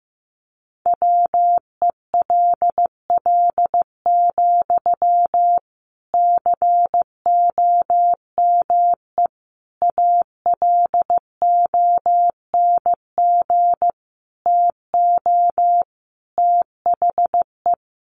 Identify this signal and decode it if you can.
Morse code, heard as a keyed tone, about 15 wpm, 700 Hz. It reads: WELL, COME ALONG TO THE
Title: Sentences from Top 1300 Words 15wpm